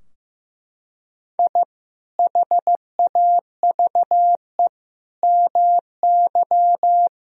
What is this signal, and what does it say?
Morse code, heard as a keyed tone, about 15 wpm, 700 Hz. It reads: I HAVE MY
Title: Sentences from Top 1000 Words - Review Part 3 - 15wpm